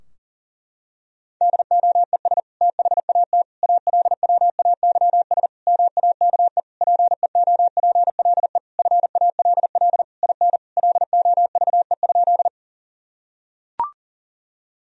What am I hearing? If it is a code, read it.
Morse code, heard as a keyed tone, about 40 wpm, 700 Hz. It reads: DOES THAT ALWAYS MAKE PEOPLE FALL IN LOVE?
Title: Sentences from Top 1000 Words - Review Part 2 - 40wpm